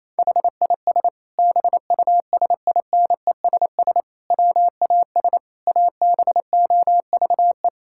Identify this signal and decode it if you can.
Morse code, heard as a keyed tone, about 28 wpm, 700 Hz. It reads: HIS BUSINESS WAS ABOVE
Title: Sentences from Top 1000 Words - Review Part 6 - 28wpm